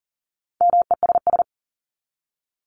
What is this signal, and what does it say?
Morse code, heard as a keyed tone, about 40 wpm, 700 Hz. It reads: MESS
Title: Top 1751-2000 Words 40wpm